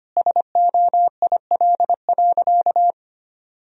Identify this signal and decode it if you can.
Morse code, heard as a keyed tone, about 25 wpm, 700 Hz. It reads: SOIL.
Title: Sentences from Top 1000 Words - Review Part 4 - 25wpm